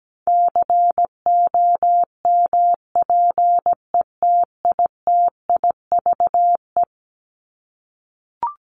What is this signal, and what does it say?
Morse code, heard as a keyed tone, about 17 wpm, 700 Hz. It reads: COMPETITIVE
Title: Top 1001-1250 Words 17wpm